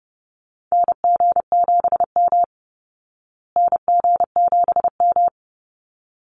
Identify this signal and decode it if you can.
Morse code, heard as a keyed tone, about 30 wpm, 700 Hz. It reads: NG7M NG7M
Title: Call Signs Repeated 3x - 30wpm